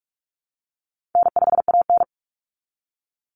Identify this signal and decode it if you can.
Morse code, heard as a keyed tone, about 44 wpm, 700 Hz. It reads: N5AN
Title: Easy Call Signs Repeated 3x - 45wpm